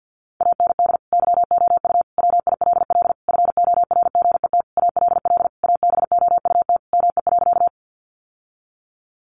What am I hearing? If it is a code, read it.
Morse code, heard as a keyed tone, about 50 wpm, 700 Hz. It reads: AND YOU WILL FORGET ALL ABOUT ME.
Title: Sentences from Top 1200 Words 50wpm